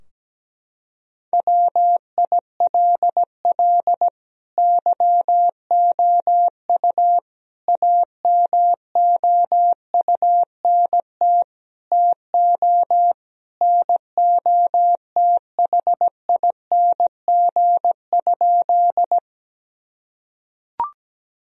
Morse code, heard as a keyed tone, about 17 wpm, 700 Hz. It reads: WILL YOU AMOUNT TO NOTHING?
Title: Sentences from Top 1000 Words - Review Part 4 - 17wpm